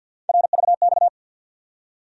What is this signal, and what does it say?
Morse code, heard as a keyed tone, about 47 wpm, 700 Hz. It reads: RVX
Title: USA Call Sign Postfixes - 3 Letters 50wpm